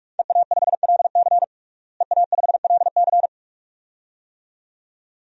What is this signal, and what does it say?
Morse code, heard as a keyed tone, about 42 wpm, 700 Hz. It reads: EA5LC EA5LC
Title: DXCC-100 Call Signs - 5 Characters - Repeated 2x - 45wpm